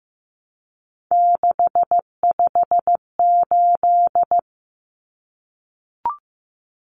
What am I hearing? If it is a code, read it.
Morse code, heard as a keyed tone, about 15 wpm, 700 Hz. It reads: 658
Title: Numbers - 3-Digit 15wpm